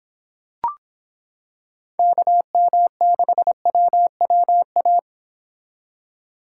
Morse code, heard as a keyed tone, about 26 wpm, 700 Hz. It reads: KM6WWA